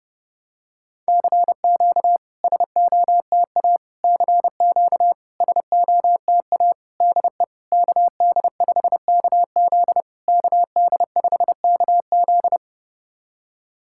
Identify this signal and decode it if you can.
Morse code, heard as a keyed tone, about 32 wpm, 700 Hz. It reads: CQ SOTA CQ SOTA DE KD5KZ KD5KZ
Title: SOTA QSO Practice 30wpm